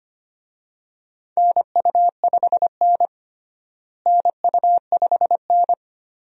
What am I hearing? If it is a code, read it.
Morse code, heard as a keyed tone, about 25 wpm, 700 Hz. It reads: NU5N NU5N
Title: USA Call Signs - 4 Characters - Repeated 3x - 25wpm